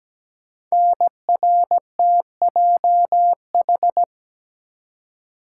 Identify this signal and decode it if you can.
Morse code, heard as a keyed tone, about 17 wpm, 700 Hz. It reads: NRTJH
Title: Code Groups - 5 Letters - 17wpm